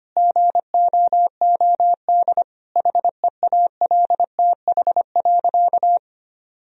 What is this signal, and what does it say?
Morse code, heard as a keyed tone, about 25 wpm, 700 Hz. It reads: GOOD HEALTH.
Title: Sentences from Top 1000 Words - Review Part 3 - 25wpm